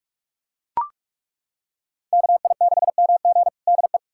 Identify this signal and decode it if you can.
Morse code, heard as a keyed tone, about 43 wpm, 700 Hz. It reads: KI6MG DE
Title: USA Callsign DE Callsign 45wpm